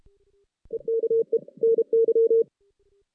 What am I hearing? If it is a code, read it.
Morse code, heard as a keyed tone, about 32 wpm, 460 Hz. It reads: EKE NY
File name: 2016 JAN CW NAQP